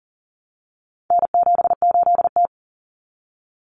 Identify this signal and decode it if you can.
Morse code, heard as a keyed tone, about 40 wpm, 700 Hz. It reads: N78T
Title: Common Cut Numbers - 4-Digit 40wpm